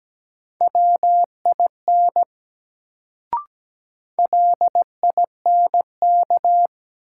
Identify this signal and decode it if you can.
Morse code, heard as a keyed tone, about 17 wpm, 700 Hz. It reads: WIN LINK
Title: Sentences from Top 1500 Words - New Word Drill Alt - 17wpm